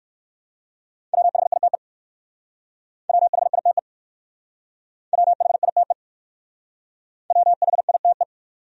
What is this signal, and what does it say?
Morse code, heard as a keyed tone, about 58 wpm, 700 Hz. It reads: WHITE WHITE WHITE WHITE